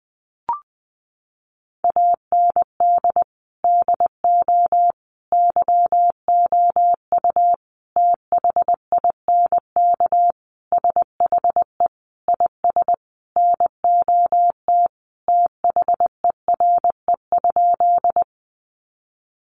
Morse code, heard as a keyed tone, about 20 wpm, 700 Hz. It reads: AND DO YOU THINK SHE IS NOT THERE?
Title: Sentences from Top 600 Words 20wpm